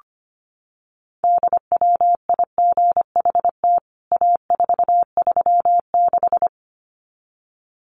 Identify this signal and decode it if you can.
Morse code, heard as a keyed tone, about 25 wpm, 700 Hz. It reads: DWIGHT A436
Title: CWT Exchange 25wpm